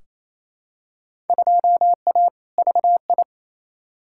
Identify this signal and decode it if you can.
Morse code, heard as a keyed tone, about 28 wpm, 700 Hz. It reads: 2A VI
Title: Field Day Exchanges Encore - 28wpm